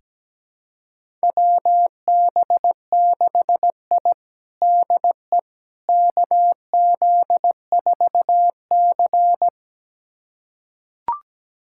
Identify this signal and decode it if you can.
Morse code, heard as a keyed tone, about 17 wpm, 700 Hz. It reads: WB6I DE KZ4C
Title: USA Callsign DE Callsign - 4 Characters - 17wpm